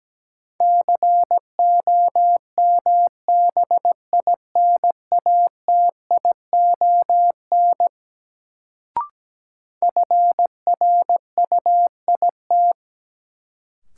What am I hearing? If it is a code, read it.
Morse code, heard as a keyed tone, about 17 wpm, 700 Hz. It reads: COMBINATION FRUIT
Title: Sentences from Top 1300 Words - New Word Drill Alt - 17wpm